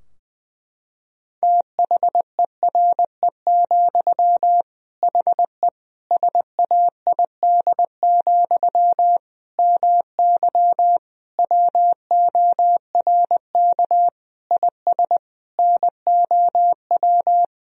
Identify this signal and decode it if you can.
Morse code, heard as a keyed tone, about 20 wpm, 700 Hz. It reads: THERE, HE SAID, MY WORK IS NOW
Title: Sentences from Top 1000 Words - Review Part 2 - 20wpm